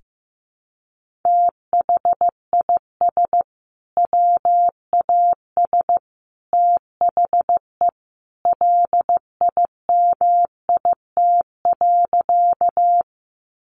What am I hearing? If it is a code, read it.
Morse code, heard as a keyed tone, about 15 wpm, 700 Hz. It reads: THIS WAS THE LIMIT.
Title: Sentences from Top 1500 Words 15wpm